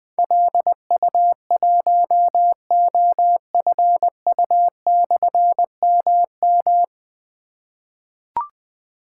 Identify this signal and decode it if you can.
Morse code, heard as a keyed tone, about 20 wpm, 700 Hz. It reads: LU1OFU/MM
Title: Difficult Call Signs 20wpm